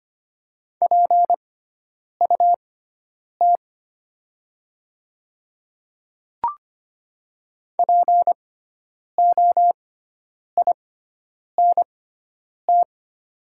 Morse code, heard as a keyed tone, about 25 wpm, spaced as for 9 wpm, 690 Hz. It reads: PUT POINT
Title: 6x Character Spacing - Top 500 Words - 25wpm